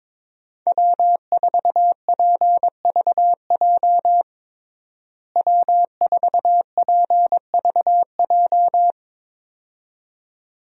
Morse code, heard as a keyed tone, about 22 wpm, 700 Hz. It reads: W4PVJ W4PVJ
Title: USA Call Signs - 5 Characters - Repeated 2x - 22wpm